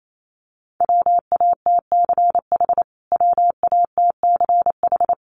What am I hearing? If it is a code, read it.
Morse code, heard as a keyed tone, about 28 wpm, 700 Hz. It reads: WATCH WATCH
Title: Top 500 Words Repeated 28wpm